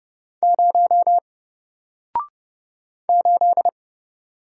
Morse code, heard as a keyed tone, about 30 wpm, 700 Hz. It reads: 0 8
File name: Single-Letter-Number-30wpm